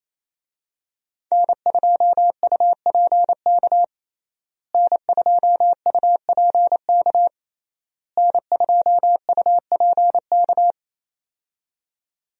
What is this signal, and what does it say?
Morse code, heard as a keyed tone, about 28 wpm, 700 Hz. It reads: N2UPK N2UPK N2UPK
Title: USA Call Signs - 5 Characters - Repeated 3x - 28wpm